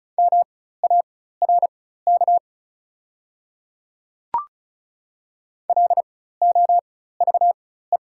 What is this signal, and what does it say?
Morse code, heard as a keyed tone, about 36 wpm, spaced as for 16 wpm, 700 Hz. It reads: MARK LOVE
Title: 4x Character Spacing - Top 500 Words - 35wpm